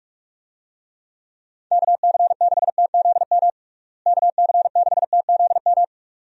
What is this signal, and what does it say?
Morse code, heard as a keyed tone, about 46 wpm, 700 Hz. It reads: KC6TZM KC6TZM
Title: USA Call Signs - 6 Characters - Repeated 2x - 45wpm